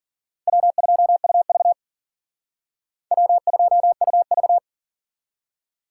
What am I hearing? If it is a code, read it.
Morse code, heard as a keyed tone, about 51 wpm, 700 Hz. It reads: W2UV W2UV